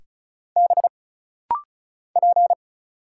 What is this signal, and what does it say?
Morse code, heard as a keyed tone, about 35 wpm, 700 Hz. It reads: B P
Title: Single Letter-Number Alt - Rapid-Fire 35wpm